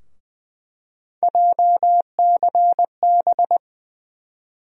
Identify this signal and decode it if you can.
Morse code, heard as a keyed tone, about 21 wpm, 700 Hz. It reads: JCB
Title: USA Call Sign Postfixes - 3 Letters 20wpm